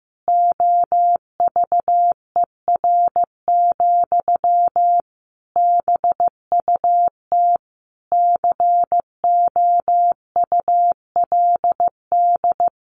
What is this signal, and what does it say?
Morse code, heard as a keyed tone, about 15 wpm, 700 Hz. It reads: OVER, BUT COULD
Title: Sentences from Top 1200 Words 15wpm